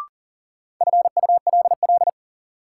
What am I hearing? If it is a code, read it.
Morse code, heard as a keyed tone, about 37 wpm, 700 Hz. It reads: FULL
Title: Sentences from Top 500 Words - New Word Drill - 40wpm